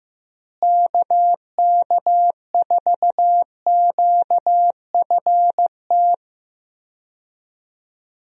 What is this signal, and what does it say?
Morse code, heard as a keyed tone, about 15 wpm, 700 Hz. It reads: KK4QFT
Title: USA Call Signs - 6 Characters - Repeated 3x - 15wpm